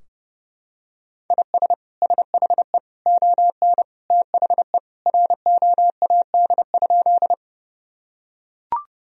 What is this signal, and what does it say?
Morse code, heard as a keyed tone, about 32 wpm, 700 Hz. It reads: IS SHE ON THE ROAD?
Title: Sentences from Top 1000 Words - Review Part 2 - 30wpm